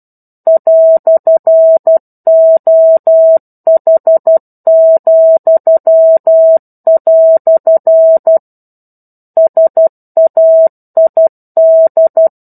Morse code, heard as a keyed tone, about 12 wpm, 650 Hz. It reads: "OH," SAID
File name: 01-Chapter-01 12 WPM